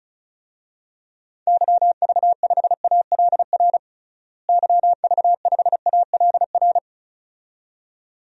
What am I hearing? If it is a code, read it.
Morse code, heard as a keyed tone, about 34 wpm, 700 Hz. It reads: YV5ALR YV5ALR
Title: DXCC-100 Call Signs - 6 Characters - Repeated 2x - 35wpm